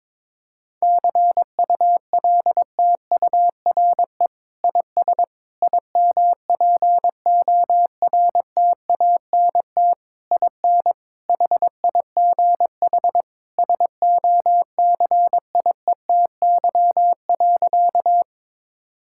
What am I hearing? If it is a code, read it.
Morse code, heard as a keyed tone, about 22 wpm, 700 Hz. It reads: CULTURE IS IMPORTANT IN HIGH SOCIETY.
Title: Sentences from Top 1000 Words - Review Part 6 - 22wpm